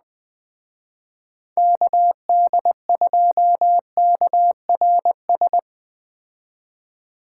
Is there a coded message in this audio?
Yes: Morse code from 1.6 to 5.6 s, heard as a keyed tone, about 20 wpm, 700 Hz. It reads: KD2KRS